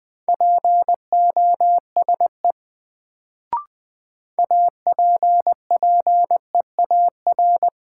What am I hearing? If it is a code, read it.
Morse code, heard as a keyed tone, about 20 wpm, 700 Hz. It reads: POSE APPEAR
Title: Sentences from Top 1000 Words - Review Part 2 - New Word Drill - 20wpm